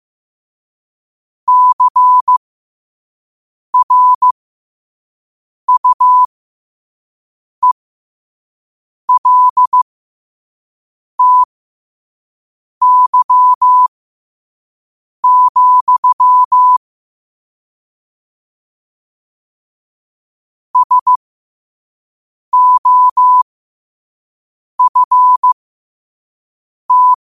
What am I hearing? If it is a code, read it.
Morse code, heard as a keyed tone, about 15 wpm, spaced as for 5 wpm, 1000 Hz. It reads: CRUELTY, SOFT
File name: qotdfiles / 5wpm